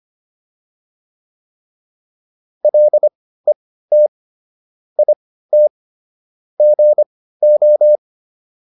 Morse code, heard as a keyed tone, about 25 wpm, spaced as for 15 wpm, 600 Hz. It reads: LET IT GO